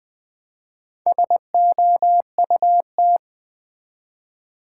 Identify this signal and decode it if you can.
Morse code, heard as a keyed tone, about 20 wpm, 700 Hz. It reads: SOUT
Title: Build A Word - 2 Letters At A Time - 8-Letter Words - 20wpm